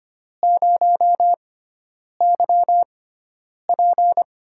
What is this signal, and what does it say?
Morse code, heard as a keyed tone, about 25 wpm, spaced as for 9 wpm, 700 Hz. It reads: 0YP
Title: USA Call Signs - 5 Characters - 6x Character Spacing - 25wpm